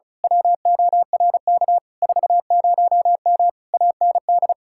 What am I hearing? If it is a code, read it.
Morse code, heard as a keyed tone, about 33 wpm, 700 Hz. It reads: WORK 40M AND
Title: Speed-Racing Common QSO Phrases - 35wpm